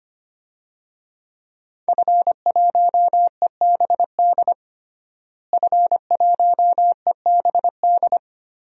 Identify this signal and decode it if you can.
Morse code, heard as a keyed tone, about 25 wpm, 700 Hz. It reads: F1EBD F1EBD
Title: DXCC-100 Call Signs - 5 Characters - Repeated 3x - 25wpm